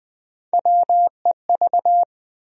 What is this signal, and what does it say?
Morse code, heard as a keyed tone, about 20 wpm, 700 Hz. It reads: WEV
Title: USA Call Sign Postfixes - 3 Letters 20wpm